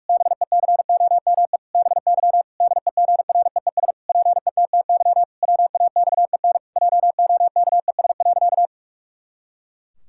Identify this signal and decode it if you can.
Morse code, heard as a keyed tone, about 46 wpm, 700 Hz. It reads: BECOME BY DEGREES PETTY WAXEN JOKES.
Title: Pangram Challenge v2 - 45wpm